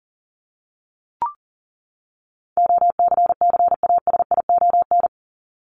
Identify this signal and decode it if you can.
Morse code, heard as a keyed tone, about 40 wpm, 700 Hz. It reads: OCCASION
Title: Top 1501-1750 Words 40wpm